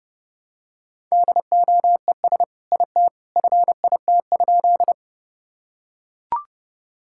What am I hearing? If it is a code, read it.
Morse code, heard as a keyed tone, about 30 wpm, 700 Hz. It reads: DOES IT FIT?
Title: Sentences from Top 1000 Words - Review Part 5 - 30wpm